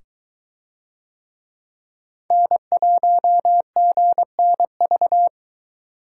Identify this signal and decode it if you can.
Morse code, heard as a keyed tone, about 20 wpm, 700 Hz. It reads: N1GNV